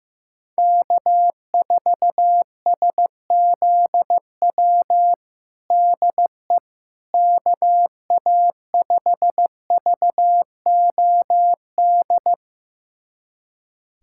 Morse code, heard as a keyed tone, about 15 wpm, 700 Hz. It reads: K4SZW DE KA5VOD
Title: USA Callsign DE Callsign 15wpm